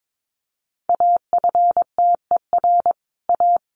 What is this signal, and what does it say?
Morse code, heard as a keyed tone, about 22 wpm, 700 Hz. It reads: AFTER A
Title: A Sound Of Thunder 22wpm